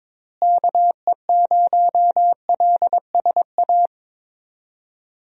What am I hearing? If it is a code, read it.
Morse code, heard as a keyed tone, about 22 wpm, 700 Hz. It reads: KE0LSA
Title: USA Call Signs - 6 Characters - Repeated 2x - 22wpm